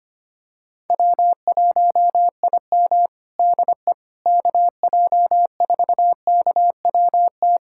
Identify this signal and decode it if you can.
Morse code, heard as a keyed tone, about 25 wpm, 700 Hz. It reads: W1IM DE KJ4KWT
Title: USA Callsign DE Callsign 25wpm